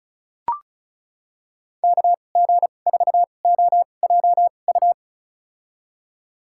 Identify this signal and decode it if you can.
Morse code, heard as a keyed tone, about 36 wpm, spaced as for 25 wpm, 700 Hz. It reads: KG4OJU
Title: USA Call Signs - 2x Character Spacing - 35wpm